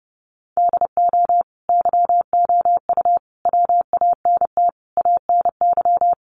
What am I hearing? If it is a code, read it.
Morse code, heard as a keyed tone, about 30 wpm, 700 Hz. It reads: DO YOU WANT ANY
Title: Sentences from Top 1400 Words 30wpm